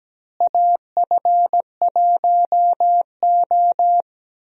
Morse code, heard as a keyed tone, about 17 wpm, 700 Hz. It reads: AF1O
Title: USA Call Signs - 4 Characters - Repeated 2x - 17wpm